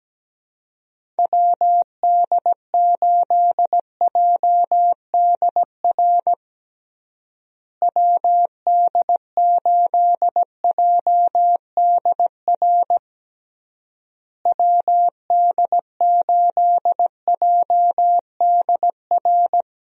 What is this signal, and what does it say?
Morse code, heard as a keyed tone, about 17 wpm, 700 Hz. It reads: WD8JDR WD8JDR WD8JDR
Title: USA Call Signs - 6 Characters - Repeated 3x - 17wpm